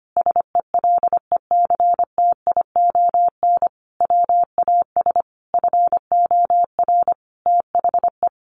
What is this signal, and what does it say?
Morse code, heard as a keyed tone, about 25 wpm, 700 Hz. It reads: SELECTION WAS FOR THE
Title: Sentences from Top 1400 Words 25wpm